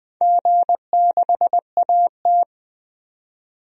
Morse code, heard as a keyed tone, about 20 wpm, 700 Hz. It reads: G6AT
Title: DXCC-100 Call Signs - 4 Characters - 20wpm